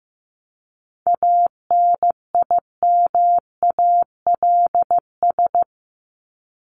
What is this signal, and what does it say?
Morse code, heard as a keyed tone, about 15 wpm, 700 Hz. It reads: ANIMALS
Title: ES:S Suffix - Top 100 Words with Base Words 15wpm